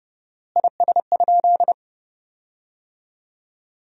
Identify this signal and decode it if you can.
Morse code, heard as a keyed tone, about 30 wpm, 700 Hz. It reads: IS?
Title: 4x Word-Spacing - Sentences from Top 100 Words - 30wpm